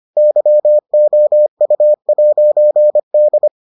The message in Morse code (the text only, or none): YOU'D